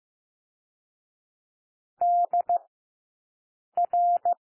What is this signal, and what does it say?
Morse code, heard as a keyed tone, about 15 wpm, 700 Hz. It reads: D R
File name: cwmp3 / W1 SOUND WORDS 15 wpm